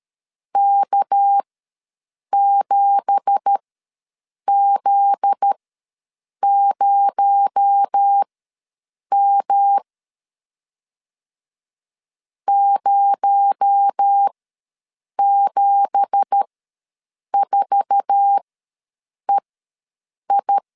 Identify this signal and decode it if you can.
Morse code, heard as a keyed tone, about 13 wpm, spaced as for 7 wpm, 800 Hz. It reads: K7Z0M 074EI
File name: LEZIONI / 9 2867 ws 12wpm 3431